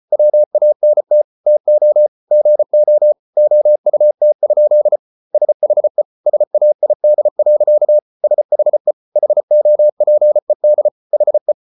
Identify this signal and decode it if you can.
Morse code, heard as a keyed tone, about 34 wpm, 600 Hz. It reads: WANT TO GO OUT? SHE SAID. SHE HOPED HE